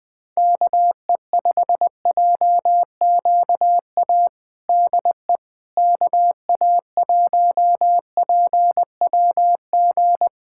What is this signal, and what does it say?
Morse code, heard as a keyed tone, about 20 wpm, 700 Hz. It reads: KE5JQA DE KA1PWG
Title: USA Callsign DE Callsign 20wpm